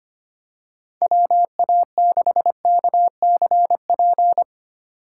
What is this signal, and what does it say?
Morse code, heard as a keyed tone, about 25 wpm, 700 Hz. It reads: WA6KCP
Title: USA Call Signs - 6 Characters - 2x Character Spacing - 25wpm